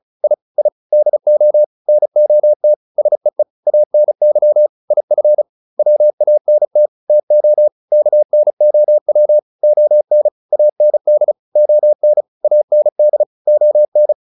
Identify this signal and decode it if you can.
Morse code, heard as a keyed tone, about 34 wpm, 600 Hz. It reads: I I DO NOT SEE ANY IF WANT TO KNOW ON AND ON AND ON